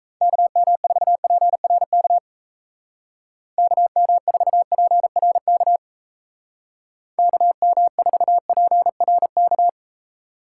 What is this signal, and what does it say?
Morse code, heard as a keyed tone, about 35 wpm, 700 Hz. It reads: KM4PRK KM4PRK KM4PRK
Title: Speed Racing - USA Call Signs 28wpm